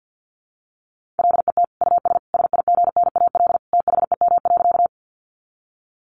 Morse code, heard as a keyed tone, about 50 wpm, 700 Hz. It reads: LET US SIGNAL THEM.
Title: Sentences from Top 1400 Words 50wpm